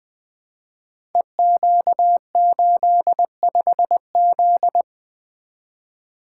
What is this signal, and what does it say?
Morse code, heard as a keyed tone, about 20 wpm, 700 Hz. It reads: EQ85Z
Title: Code Groups - 5 Letter:Numbers - 20wpm